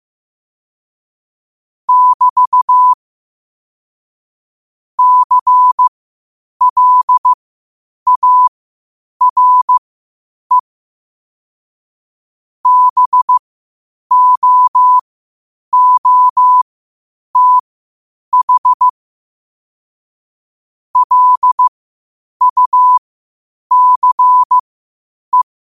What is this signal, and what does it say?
Morse code, heard as a keyed tone, about 15 wpm, spaced as for 8 wpm, 1000 Hz. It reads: <BT> CLARE BOOTH LUCE
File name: qotdfiles / 8wpm